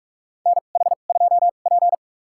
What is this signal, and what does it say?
Morse code, heard as a keyed tone, about 47 wpm, spaced as for 33 wpm, 700 Hz. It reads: NH2P
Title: USA Call Signs - 4 Characters - 2x Character Spacing - 45wpm